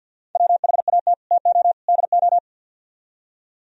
Morse code, heard as a keyed tone, about 60 wpm, 700 Hz. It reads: WHAT TO DO
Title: Speed-Racing - Sets of 3 Words 50wpm